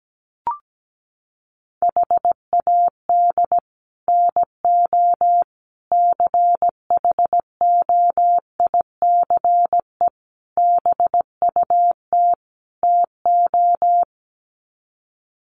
Morse code, heard as a keyed tone, about 17 wpm, 700 Hz. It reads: HAD NO CHOICE BUT TO
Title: Sets of 5 Words 17wpm